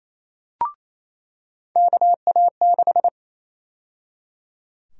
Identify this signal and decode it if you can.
Morse code, heard as a keyed tone, about 28 wpm, 700 Hz. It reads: KA6
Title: USA Call Sign Prefixes - 28wpm